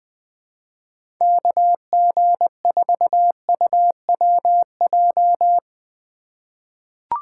Morse code, heard as a keyed tone, about 20 wpm, 700 Hz. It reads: KG4UWJ
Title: USA Call Signs - 6 Characters - 4x Character Spacing - 20wpm